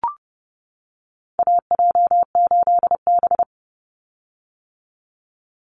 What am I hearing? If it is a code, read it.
Morse code, heard as a keyed tone, about 30 wpm, 700 Hz. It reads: AJ8B
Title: Call Signs - 30wpm